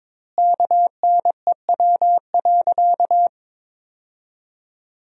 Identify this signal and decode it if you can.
Morse code, heard as a keyed tone, about 22 wpm, 700 Hz. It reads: KNEW.
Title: 2x Word-Spacing - Sentences from Top 500 Words - 22wpm